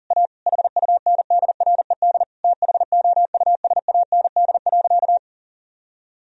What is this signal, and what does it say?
Morse code, heard as a keyed tone, about 37 wpm, 700 Hz. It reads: A HUNDRED THOUSAND.
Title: Sentences from Top 1000 Words - Review Part 3 - 40wpm